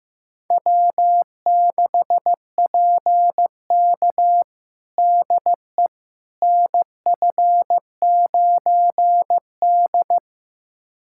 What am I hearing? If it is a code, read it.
Morse code, heard as a keyed tone, about 15 wpm, 700 Hz. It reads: W6PK DE NF9D
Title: USA Callsign DE Callsign - 4 Characters - 15wpm